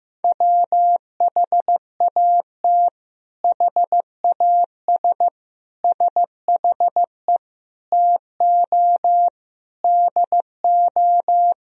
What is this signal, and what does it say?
Morse code, heard as a keyed tone, about 16 wpm, 700 Hz. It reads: WHAT HAS SHE TO DO
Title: Sentences from Top 100 Words and Abbreviations 15wpm